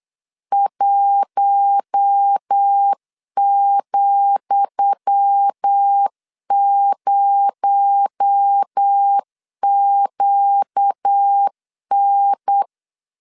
Morse code, heard as a keyed tone, about 8 wpm, 800 Hz. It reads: 1,0QN
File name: LEZIONI / 10 SIGNS / 10D COMA 8wpm 3749